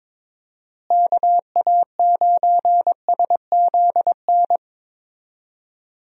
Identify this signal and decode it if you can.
Morse code, heard as a keyed tone, about 22 wpm, 700 Hz. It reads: KA9SZN
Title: USA Call Signs - 6 Characters - Repeated 2x - 22wpm